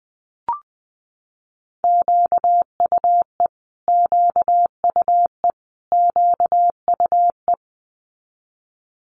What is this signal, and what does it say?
Morse code, heard as a keyed tone, about 19 wpm, 700 Hz. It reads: QUE QUE QUE
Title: Multi-letter Phonemes Repeated 3x 20wpm